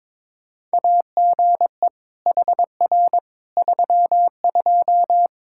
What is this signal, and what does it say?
Morse code, heard as a keyed tone, about 21 wpm, 700 Hz. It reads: AGE HR 32
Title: Practice QSO - Exchange 3 - More Info - 22wpm